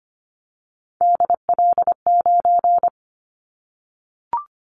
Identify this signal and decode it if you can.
Morse code, heard as a keyed tone, about 25 wpm, 700 Hz. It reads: DL9
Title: Callsign Prefixes for Top 25 DXCC - Repeated 3x - 25wpm